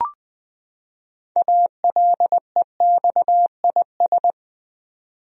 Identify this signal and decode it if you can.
Morse code, heard as a keyed tone, about 20 wpm, 700 Hz. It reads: ALEXIS
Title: USA Common Names - 20wpm